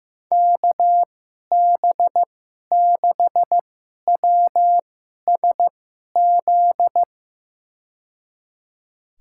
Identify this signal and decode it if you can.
Morse code, heard as a keyed tone, about 15 wpm, spaced as for 11 wpm, 700 Hz. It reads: KB6WSZ
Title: USA Call Signs - 6 Characters - 2x Character Spacing - 15wpm